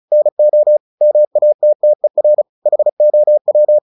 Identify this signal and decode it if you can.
Morse code, heard as a keyed tone, about 37 wpm, 600 Hz. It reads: NO MATTER HOW
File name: the old man and the sea / the old man and the sea 1 35wpm